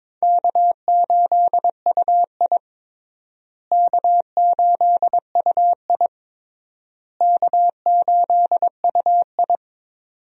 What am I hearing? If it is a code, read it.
Morse code, heard as a keyed tone, about 22 wpm, 700 Hz. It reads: K8UI K8UI K8UI
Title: USA Call Signs - 4 Characters - Repeated 3x - 22wpm